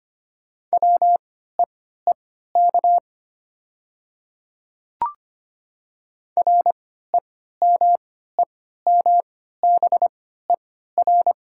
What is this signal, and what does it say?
Morse code, heard as a keyed tone, about 25 wpm, spaced as for 14 wpm, 700 Hz. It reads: WEEK REMEMBER
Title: 3x Character Spacing - Top 500 Words - 25wpm